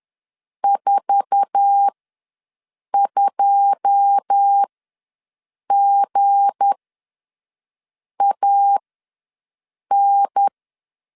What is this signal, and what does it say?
Morse code, heard as a keyed tone, about 10 wpm, spaced as for 5 wpm, 800 Hz. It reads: 42GAN
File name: LEZIONI / 9 2 ws 10wpm 4130